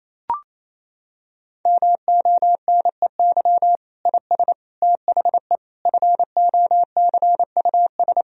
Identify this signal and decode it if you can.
Morse code, heard as a keyed tone, about 28 wpm, 700 Hz. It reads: MONEY IS THE FOCUS
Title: Sentences from Top 1000 Words - Review Part 4 - 28wpm